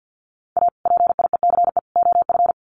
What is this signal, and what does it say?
Morse code, heard as a keyed tone, about 50 wpm, 700 Hz. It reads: A PIECE OF
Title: Sets of 3 Words 50wpm